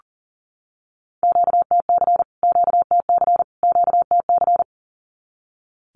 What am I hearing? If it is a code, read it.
Morse code, heard as a keyed tone, about 40 wpm, 700 Hz. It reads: QTC QTC QTC
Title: Q-Codes Repeated 3x 40wpm